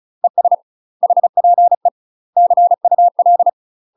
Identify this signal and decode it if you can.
Morse code, heard as a keyed tone, about 34 wpm, 700 Hz. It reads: ES HPE CUL
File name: UK MASTER QSOs TRAINING / 35WPM / UK MASTER QSOs 0039